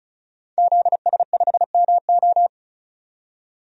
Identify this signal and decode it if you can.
Morse code, heard as a keyed tone, about 34 wpm, 700 Hz. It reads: ZS5MO
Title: DXCC-100 Call Signs - 5 Characters - 35wpm